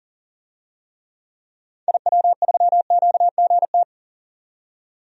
Morse code, heard as a keyed tone, about 40 wpm, 700 Hz. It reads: IW3QGT
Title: DXCC-100 Call Signs - 6 Characters - 40wpm